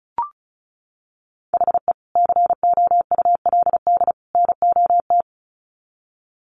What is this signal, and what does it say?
Morse code, heard as a keyed tone, about 34 wpm, 700 Hz. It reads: HE COULD NOT
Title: Sets of 3 Words 35wpm